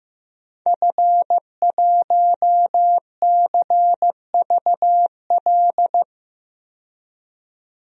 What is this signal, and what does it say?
Morse code, heard as a keyed tone, about 15 wpm, 700 Hz. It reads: F1CVL
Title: DXCC-100 Call Signs - 5 Characters - 15wpm